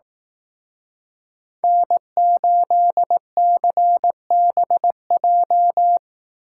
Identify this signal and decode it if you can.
Morse code, heard as a keyed tone, about 18 wpm, 700 Hz. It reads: N8CBJ
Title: Speed Racing - USA 1x3 Call Signs 15wpm